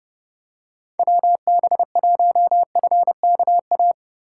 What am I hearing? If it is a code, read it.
Morse code, heard as a keyed tone, about 29 wpm, 700 Hz. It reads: WB1FKA